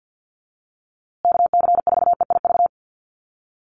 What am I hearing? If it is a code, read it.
Morse code, heard as a keyed tone, about 50 wpm, 700 Hz. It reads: KC4EIV
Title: USA Call Signs Repeated 3x 50wpm